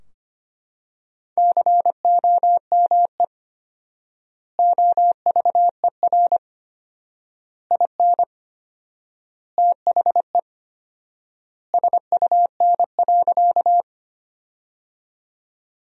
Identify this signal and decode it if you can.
Morse code, heard as a keyed tone, about 25 wpm, 700 Hz. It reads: COME OVER IN THE SUN.
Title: 4x Word-Spacing - Sentences from Top 300 Words - 25wpm